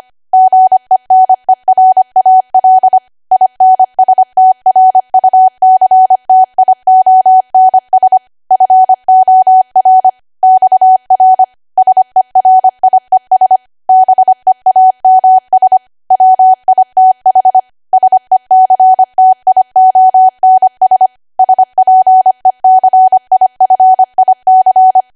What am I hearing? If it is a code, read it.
Morse code, heard as a keyed tone, about 25 wpm, 750 Hz. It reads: GENERAL INSTRUCTIONS FOR XR SERIES BEAMS WITH SECTIONS SPECIFIC
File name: Archive / 25 WPM / 25WPM